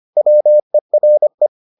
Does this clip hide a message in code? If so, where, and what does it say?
Yes: Morse code from 0.2 to 1.5 s, heard as a keyed tone, about 25 wpm, 600 Hz. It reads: WERE